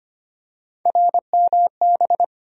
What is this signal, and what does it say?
Morse code, heard as a keyed tone, about 25 wpm, 700 Hz. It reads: RMB
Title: USA Call Sign Postfixes - 3 Letters 25wpm